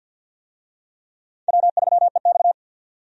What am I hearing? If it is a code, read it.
Morse code, heard as a keyed tone, about 52 wpm, 700 Hz. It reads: W3EX
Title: USA Call Signs - 4 Characters - 50wpm